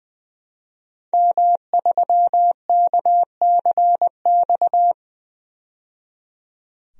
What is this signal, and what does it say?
Morse code, heard as a keyed tone, about 20 wpm, 700 Hz. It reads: M3KCX
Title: DXCC-100 Call Signs - 5 Characters - 20wpm